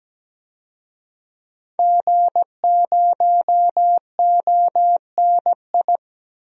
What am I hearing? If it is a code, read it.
Morse code, heard as a keyed tone, about 17 wpm, 700 Hz. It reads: G0ONI
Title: DXCC-100 Call Signs - 5 Characters - 17wpm